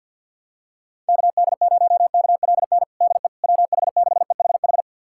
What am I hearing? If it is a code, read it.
Morse code, heard as a keyed tone, about 51 wpm, 700 Hz. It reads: KD0KLN DE WH6EHH
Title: USA Callsign DE Callsign - 6 Characters - 50wpm